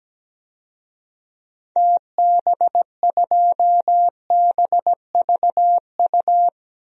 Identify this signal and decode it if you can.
Morse code, heard as a keyed tone, about 17 wpm, 700 Hz. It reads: TB2BVU
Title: DXCC-100 Call Signs - 6 Characters - 17wpm